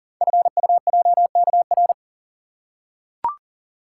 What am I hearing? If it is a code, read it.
Morse code, heard as a keyed tone, about 41 wpm, 700 Hz. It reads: FUJKR
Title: Code Groups - 5 Letters - 40wpm